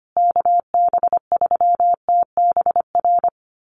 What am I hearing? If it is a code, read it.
Morse code, heard as a keyed tone, about 25 wpm, 700 Hz. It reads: KB3TBR
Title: USA Call Signs Repeated 3x 25wpm